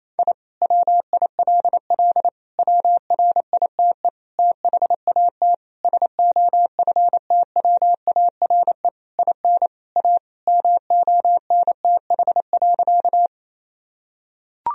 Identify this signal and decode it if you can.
Morse code, heard as a keyed tone, about 28 wpm, 700 Hz. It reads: I WILL WRITE THAT SOFTWARE IN A MONTH.
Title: Sentences from Top 1000 Words - Review Part 3 - 28wpm